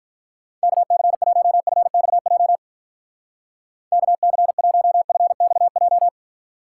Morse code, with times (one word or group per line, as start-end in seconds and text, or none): 0.6-2.6 KC1FXJ
3.9-6.1 KC1FXJ